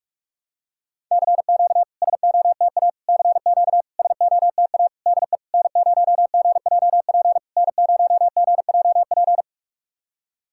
Morse code, heard as a keyed tone, about 46 wpm, 690 Hz. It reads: CQ SOTA CQ SOTA DE N0GJP N0GJP
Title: SOTA QSO Practice 45wpm